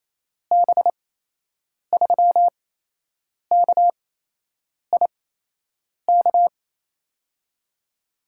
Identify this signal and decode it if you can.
Morse code, heard as a keyed tone, about 28 wpm, spaced as for 8 wpm, 700 Hz. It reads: B3KIK
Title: USA Call Signs - 6 Characters - 8x Character Spacing - 28wpm